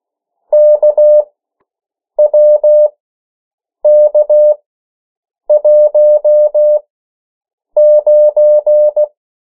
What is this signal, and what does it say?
Morse code, heard as a keyed tone, about 16 wpm, spaced as for 7 wpm, 600 Hz. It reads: KWK19
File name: morse mp3 / 14 urok 13